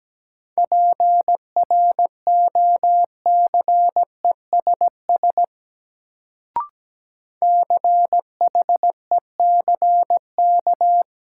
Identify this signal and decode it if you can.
Morse code, heard as a keyed tone, about 17 wpm, 700 Hz. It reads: PROCESS CHECK
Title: Sentences from Top 1000 Words - Review Part 3 - New Word Drill - 17wpm